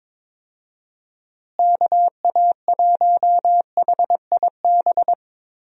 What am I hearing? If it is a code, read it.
Morse code, heard as a keyed tone, about 21 wpm, 700 Hz. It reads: KA1HIB
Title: USA Call Signs - 6 Characters - Repeated 2x - 22wpm